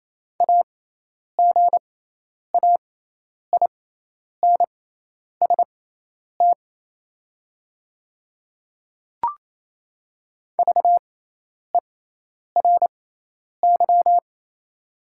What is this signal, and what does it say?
Morse code, heard as a keyed tone, about 28 wpm, spaced as for 10 wpm, 700 Hz. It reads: AGAINST VERY
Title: 6x Character Spacing - Top 500 Words - 28wpm